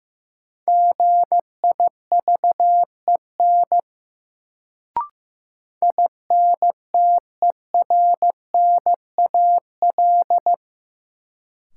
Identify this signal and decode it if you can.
Morse code, heard as a keyed tone, about 15 wpm, 700 Hz. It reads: GIVEN INTERNAL
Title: Sentences from Top 1400 Words - New Word Drill Alt - 15wpm